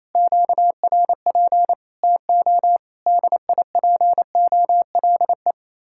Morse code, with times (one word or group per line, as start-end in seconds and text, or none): 0.1-1.7 QRP
2.0-2.8 TO
3.1-5.5 DIPOLE